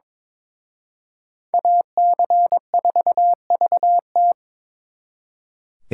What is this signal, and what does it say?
Morse code, heard as a keyed tone, about 26 wpm, 700 Hz. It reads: AC4VT
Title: Speed Racing - USA 2x2 Call Signs 22wpm